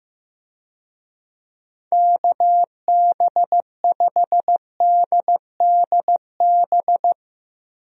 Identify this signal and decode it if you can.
Morse code, heard as a keyed tone, about 16 wpm, 700 Hz. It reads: KB5DDB
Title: USA Call Signs - 6 Characters - Repeated 2x - 15wpm